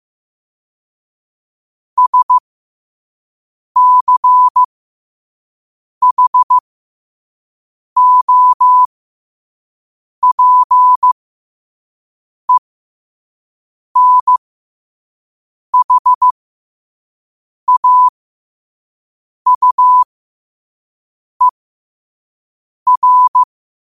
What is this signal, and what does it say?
Morse code, heard as a keyed tone, about 15 wpm, spaced as for 5 wpm, 1000 Hz. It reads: SCHOPENHAUER
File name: qotdfiles / 5wpm